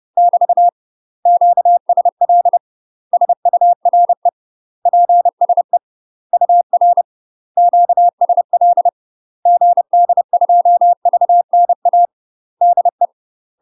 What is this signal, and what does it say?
Morse code, heard as a keyed tone, about 30 wpm, 700 Hz. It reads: <BT> QSL SURE PSE UR QSL GD2VNA DE
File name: UK MASTER QSOs TRAINING / 30WPM / UK MASTER QSOs 0026